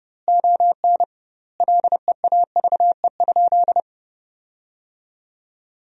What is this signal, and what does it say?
Morse code, heard as a keyed tone, about 32 wpm, 700 Hz. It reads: ON LEAVE?
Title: 2x Word-Spacing - Sentences from Top 400 Words - 30wpm